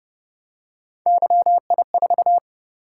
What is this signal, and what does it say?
Morse code, heard as a keyed tone, about 30 wpm, 700 Hz. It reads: YI4
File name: Speed-Racing - Three Letter-Number Combinations - 25wpm